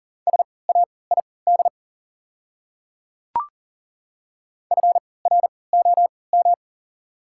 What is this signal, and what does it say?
Morse code, heard as a keyed tone, about 40 wpm, spaced as for 23 wpm, 690 Hz. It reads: SAID FROM
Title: 3x Character Spacing - Top 100 Words - 40wpm